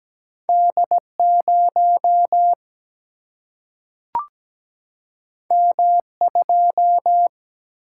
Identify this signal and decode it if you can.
Morse code, heard as a keyed tone, about 17 wpm, 700 Hz. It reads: D0 M2
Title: Single-Letter Single-Number 17wpm